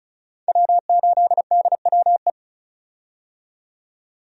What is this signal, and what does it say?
Morse code, heard as a keyed tone, about 34 wpm, 700 Hz. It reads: W8DWE
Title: USA Call Signs - 5 Characters - 35wpm